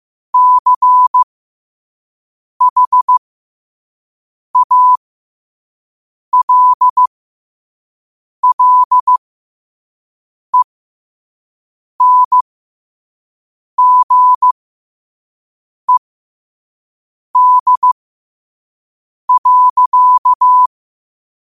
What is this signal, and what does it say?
Morse code, heard as a keyed tone, about 15 wpm, spaced as for 5 wpm, 1000 Hz. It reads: CHALLENGED.
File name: qotdfiles / 5wpm